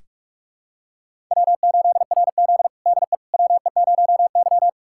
Morse code, heard as a keyed tone, about 42 wpm, 700 Hz. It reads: W8RZ DE WE0Y
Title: USA Callsign DE Callsign - 4 Characters - 45wpm